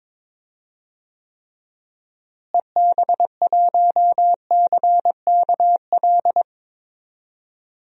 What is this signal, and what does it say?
Morse code, heard as a keyed tone, about 22 wpm, 700 Hz. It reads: EB1CKL
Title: DXCC-100 Call Signs - 6 Characters - Repeated 2x - 22wpm